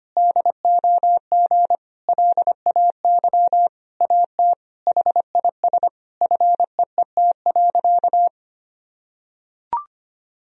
Morse code, heard as a keyed tone, about 25 wpm, 700 Hz. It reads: DOG LAY AT HIS FEET.
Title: Sentences from Top 1000 Words - Review Part 2 - 25wpm